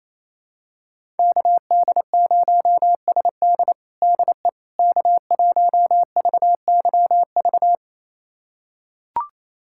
Morse code, heard as a keyed tone, about 28 wpm, 700 Hz. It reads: KD0SD DE K1VYV
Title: USA Callsign DE Callsign 28wpm